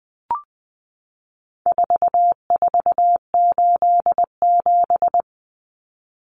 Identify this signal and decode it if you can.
Morse code, heard as a keyed tone, about 20 wpm, 700 Hz. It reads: 4487
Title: Numbers - 4-Digit 20wpm